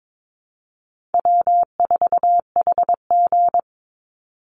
Morse code, heard as a keyed tone, about 22 wpm, 700 Hz. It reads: W4HG
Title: Easy Call Signs Repeated 2x - 22wpm